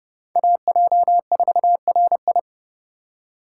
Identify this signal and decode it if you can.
Morse code, heard as a keyed tone, about 30 wpm, 700 Hz. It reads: AJ4RI